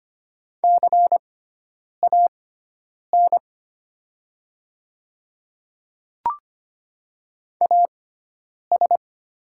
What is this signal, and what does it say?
Morse code, heard as a keyed tone, about 25 wpm, spaced as for 9 wpm, 700 Hz. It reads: CAN AS
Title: 6x Character Spacing - Top 200 Words - 25wpm